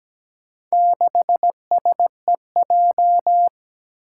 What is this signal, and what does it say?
Morse code, heard as a keyed tone, about 17 wpm, 700 Hz. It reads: 6SEJ
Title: Four Challenging Letter-Number Combinations 17wpm